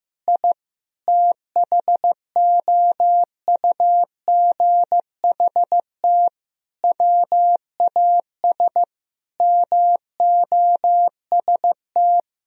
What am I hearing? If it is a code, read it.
Morse code, heard as a keyed tone, about 16 wpm, 700 Hz. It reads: I THOUGHT WAS MOST
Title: Sentences from Top 1000 Words - Review Part 4 - 15wpm